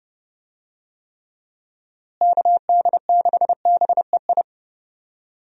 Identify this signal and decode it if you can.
Morse code, heard as a keyed tone, about 30 wpm, 700 Hz. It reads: KD6BEI
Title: USA Call Signs - 6 Characters - Repeated 3x - 30wpm